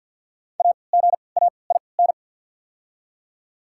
Morse code, heard as a keyed tone, about 48 wpm, 690 Hz. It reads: A G A I N
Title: 3x Character Spacing - Top 300 Words - 50wpm